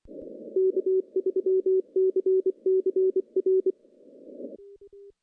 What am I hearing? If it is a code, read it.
Morse code, heard as a keyed tone, about 23 wpm, 380 Hz. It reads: K3CCR
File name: CW-SS-k5tr